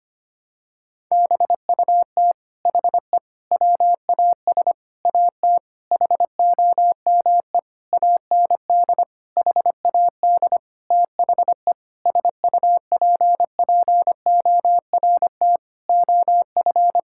Morse code, heard as a keyed tone, about 25 wpm, 700 Hz. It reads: BUT HE WAS AT HOME AND HAD THE SUPPORT OF
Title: Sentences from Top 1000 Words - Review Part 5 - 25wpm